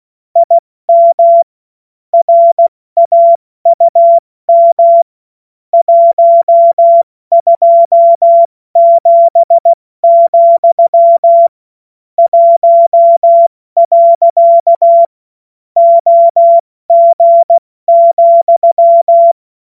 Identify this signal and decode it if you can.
Morse code, heard as a keyed tone, about 16 wpm, 680 Hz. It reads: IM RAUM 127, 1. OG,
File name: Morsekurs / Witze / Witz 23. Dezember 2025 70 F80